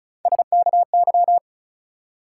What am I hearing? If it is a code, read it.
Morse code, heard as a keyed tone, about 34 wpm, 700 Hz. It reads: SKY
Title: Sentences from Top 1200 Words - New Word Drill - 35wpm